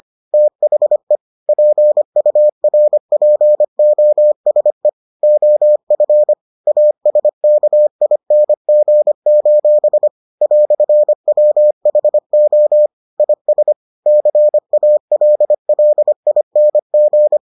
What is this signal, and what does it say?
Morse code, heard as a keyed tone, about 25 wpm, 600 Hz. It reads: THE PURPOSE OF ASKING: "WHO IS CALLING